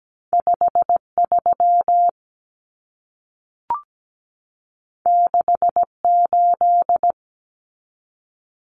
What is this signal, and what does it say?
Morse code, heard as a keyed tone, about 17 wpm, 700 Hz. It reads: 53 68
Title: Common Cut Numbers - 2-Digit 17wpm